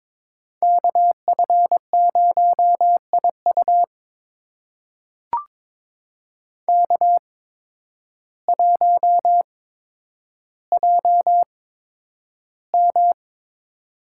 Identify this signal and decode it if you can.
Morse code, heard as a keyed tone, about 22 wpm, spaced as for 10 wpm, 700 Hz. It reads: KF0IU K1JM
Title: USA Call Signs - 8x Character Spacing - 22wpm